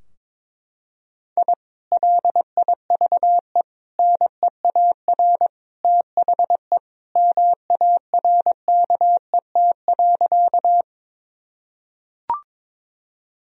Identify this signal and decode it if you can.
Morse code, heard as a keyed tone, about 21 wpm, 700 Hz. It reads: I LIVE NEAR THE MARKET.
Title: Sentences from Top 1000 Words - Review Part 3 - 22wpm